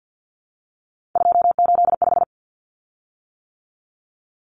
Morse code, heard as a keyed tone, about 50 wpm, 700 Hz. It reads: 285
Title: Common Cut Numbers - 3-Digit 50wpm